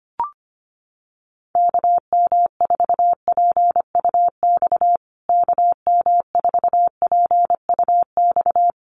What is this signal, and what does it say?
Morse code, heard as a keyed tone, about 25 wpm, 700 Hz. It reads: KM4PUX KM4PUX
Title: USA Call Signs Repeated 3x 25wpm